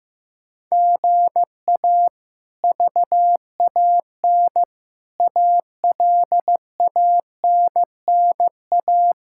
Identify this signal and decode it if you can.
Morse code, heard as a keyed tone, about 15 wpm, 700 Hz. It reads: GA VAN ALANNA
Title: K1USN SST QSO Practice - 15wpm